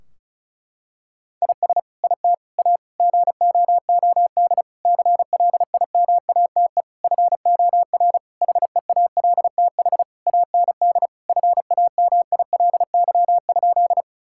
Morse code, heard as a keyed tone, about 34 wpm, 700 Hz. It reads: IS IT A GOOD CLIMATE FOR HEALTH AND FAMILY?
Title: Sentences from Top 1600 Words 35wpm